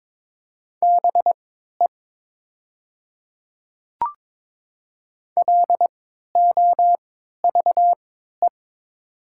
Morse code, heard as a keyed tone, about 22 wpm, spaced as for 12 wpm, 700 Hz. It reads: BE LOVE
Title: 3x Character Spacing - Top 500 Words - 22wpm